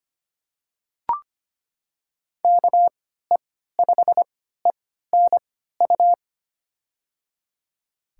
Morse code, heard as a keyed tone, about 25 wpm, spaced as for 14 wpm, 700 Hz. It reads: KE5ENU
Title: USA Call Signs - 6 Characters - 3x Character Spacing - 25wpm